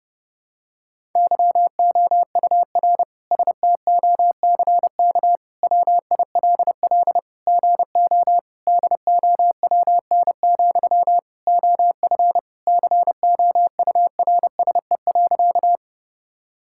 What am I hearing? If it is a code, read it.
Morse code, heard as a keyed tone, about 30 wpm, 700 Hz. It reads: YOUR STOCK WILL GO DOWN, OF COURSE.
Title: Sentences from Top 1000 Words - Review Part 5 - 30wpm